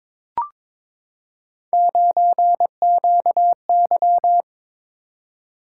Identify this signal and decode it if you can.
Morse code, heard as a keyed tone, about 22 wpm, 700 Hz. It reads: 9QY
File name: practice / 22wpm / Three Letter-Number Combinations - 22wpm